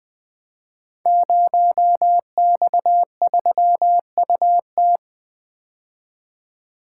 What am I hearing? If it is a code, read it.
Morse code, heard as a keyed tone, about 20 wpm, 700 Hz. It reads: 0X3UT
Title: Code Groups - 5 Letter:Numbers - 20wpm